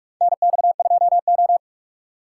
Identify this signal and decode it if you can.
Morse code, heard as a keyed tone, about 42 wpm, 700 Hz. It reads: NX2O
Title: USA Call Signs - 4 Characters - 4x Character Spacing - 45wpm